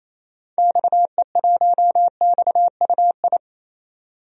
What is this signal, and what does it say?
Morse code, heard as a keyed tone, about 28 wpm, 700 Hz. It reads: XE1XUI
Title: DXCC-100 Call Signs - 6 Characters - 28wpm